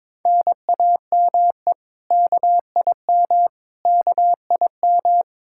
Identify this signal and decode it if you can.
Morse code, heard as a keyed tone, about 22 wpm, 700 Hz. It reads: NAME KIM KIM
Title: K1USN SST QSO Practice 22wpm